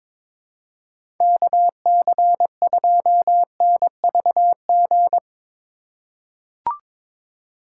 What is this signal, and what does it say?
Morse code, heard as a keyed tone, about 23 wpm, 700 Hz. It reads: KC2NVG
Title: USA Call Signs - 6 Characters - 3x Character Spacing - 22wpm